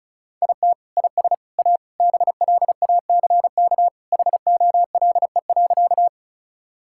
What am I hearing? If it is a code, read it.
Morse code, heard as a keyed tone, about 34 wpm, 700 Hz. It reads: IT IS A BLACK HOLE.
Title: Sentences from Top 1600 Words 35wpm